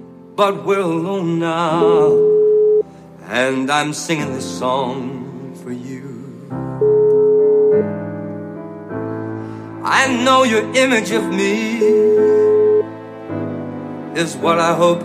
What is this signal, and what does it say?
Ringback tone (Europe, 425 Hz), with noise.